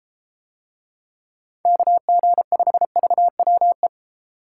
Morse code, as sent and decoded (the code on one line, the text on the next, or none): -.- --. ..... ...- .-- .
KG5VWE